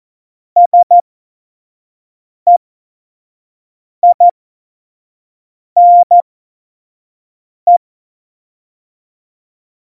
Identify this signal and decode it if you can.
Morse code, heard as a keyed tone, about 14 wpm, spaced as for 5 wpm, 700 Hz. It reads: SEINE
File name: Morsekurs / Witze / Witz 28. Januar 2020, 25 F70